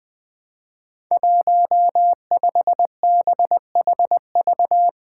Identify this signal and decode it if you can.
Morse code, heard as a keyed tone, about 20 wpm, 700 Hz. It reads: 15BHV
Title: Five Challenging Letter-Number Combinations 20wpm